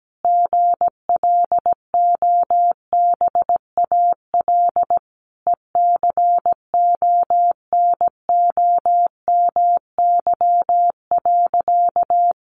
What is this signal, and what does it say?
Morse code, heard as a keyed tone, about 17 wpm, 700 Hz. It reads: GLOBAL ECONOMY.
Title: Sentences from Top 1200 Words 17wpm